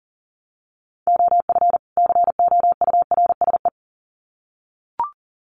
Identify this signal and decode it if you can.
Morse code, heard as a keyed tone, about 40 wpm, 700 Hz. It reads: OF COURSE
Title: Sets of 2 Words 40wpm